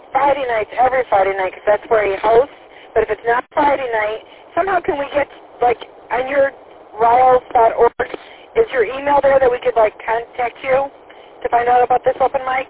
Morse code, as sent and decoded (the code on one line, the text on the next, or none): none